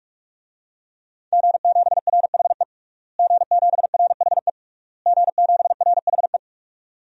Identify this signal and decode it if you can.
Morse code, heard as a keyed tone, about 46 wpm, 700 Hz. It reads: G7RHE G7RHE G7RHE
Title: DXCC-100 Call Signs - 5 Characters - Repeated 3x - 45wpm